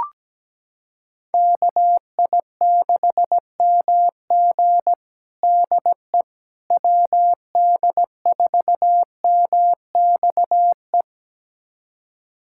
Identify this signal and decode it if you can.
Morse code, heard as a keyed tone, about 17 wpm, 700 Hz. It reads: KI6MG DE WD4MXE
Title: USA Callsign DE Callsign 17wpm